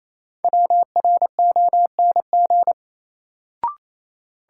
Morse code, heard as a keyed tone, about 28 wpm, 700 Hz. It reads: WRONG
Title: Sets of 5 Words Encore - New Word Drill - 28wpm